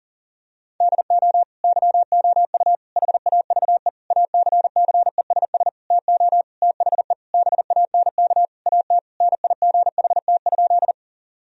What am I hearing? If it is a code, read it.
Morse code, heard as a keyed tone, about 37 wpm, 700 Hz. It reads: DO YOU HAVE ACCESS TO THE BANK AT NIGHT?
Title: Sentences from Top 1000 Words - Review Part 4 - 40wpm